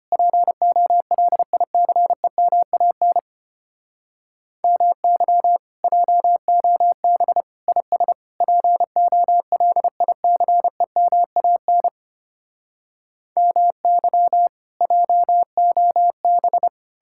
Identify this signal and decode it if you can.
Morse code, heard as a keyed tone, about 33 wpm, 700 Hz. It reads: POLICEMAN MY JOB IS POLICEMAN MY JOB